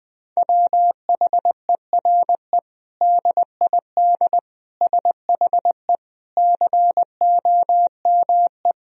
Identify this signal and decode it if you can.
Morse code, heard as a keyed tone, about 20 wpm, 700 Hz. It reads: WHERE DID SHE COME
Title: Sentences from Top 200 Words and Abbreviations 20wpm